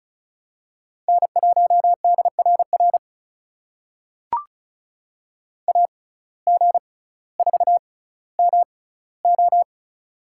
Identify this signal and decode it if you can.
Morse code, heard as a keyed tone, about 36 wpm, spaced as for 18 wpm, 700 Hz. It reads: N1DRR AG4MO
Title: USA Call Signs - 5 Characters - 6x Character Spacing - 35wpm